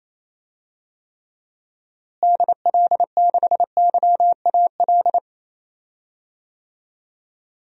Morse code, heard as a keyed tone, about 28 wpm, 700 Hz. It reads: DL6YAL
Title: DXCC-100 Call Signs - 6 Characters - Repeated 3x - 28wpm